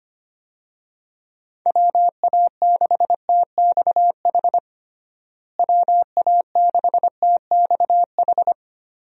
Morse code, heard as a keyed tone, about 25 wpm, 700 Hz. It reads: WA6TXH WA6TXH
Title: USA Call Signs - 6 Characters - Repeated 2x - 25wpm